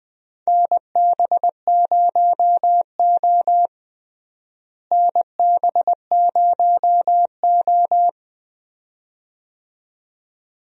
Morse code, heard as a keyed tone, about 20 wpm, 700 Hz. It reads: NB0O NB0O
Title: USA Call Signs - 4 Characters - Repeated 2x - 20wpm